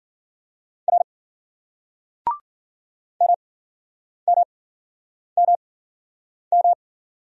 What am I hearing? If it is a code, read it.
Morse code, heard as a keyed tone, about 51 wpm, spaced as for 9 wpm, 700 Hz. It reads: R MMMM